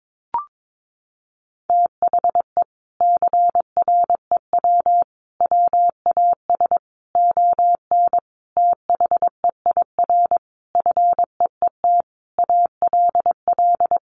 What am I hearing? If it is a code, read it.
Morse code, heard as a keyed tone, about 22 wpm, 700 Hz. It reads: THE CREW WAS ON THEIR FEET ALL
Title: Sentences from Top 1500 Words 22wpm